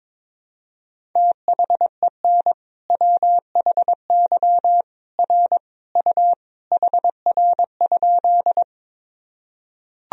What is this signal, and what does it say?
Morse code, heard as a keyed tone, about 21 wpm, 700 Hz. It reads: THEN WHY R U HR?
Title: Sentences from Top 200 Words and Abbreviations 22wpm